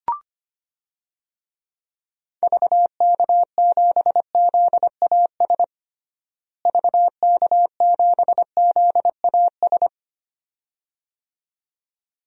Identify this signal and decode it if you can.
Morse code, heard as a keyed tone, about 25 wpm, 700 Hz. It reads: VK7ZAS VK7ZAS
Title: DXCC-100 Call Signs - 6 Characters - Repeated 2x - 25wpm